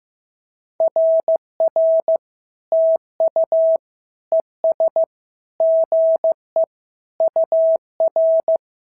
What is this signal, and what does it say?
Morse code, heard as a keyed tone, about 14 wpm, 650 Hz. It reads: RR TU ES GE UR
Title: SOTA QSO Practice 15wpm